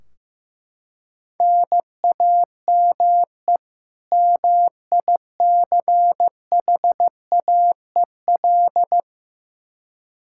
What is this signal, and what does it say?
Morse code, heard as a keyed tone, about 15 wpm, 700 Hz. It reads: NAME MICHAEL
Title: Practice QSO - Exchange 1 - Beginning - 15wpm